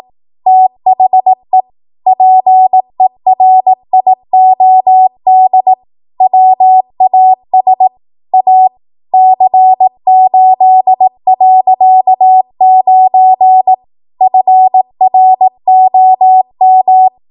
Morse code, heard as a keyed tone, about 18 wpm, 750 Hz. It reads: THE PERIOD WAS A C8.9 FROM
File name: Morse / 18WPM